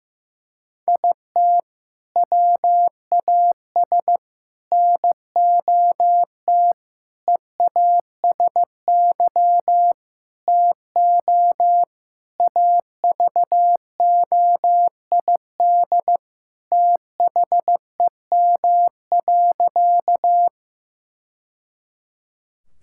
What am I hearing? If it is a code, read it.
Morse code, heard as a keyed tone, about 15 wpm, 700 Hz. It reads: IT WAS NOT EASY TO AVOID THEM.
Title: Sentences from Top 1000 Words - Review Part 6 - 15wpm